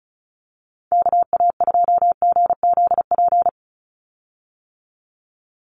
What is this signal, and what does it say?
Morse code, heard as a keyed tone, about 34 wpm, 700 Hz. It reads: KA2GZP
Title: USA Call Signs 35wpm